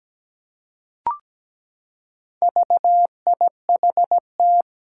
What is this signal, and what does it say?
Morse code, heard as a keyed tone, about 17 wpm, 700 Hz. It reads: VIHT
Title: Four Challenging Letter-Number Combinations 17wpm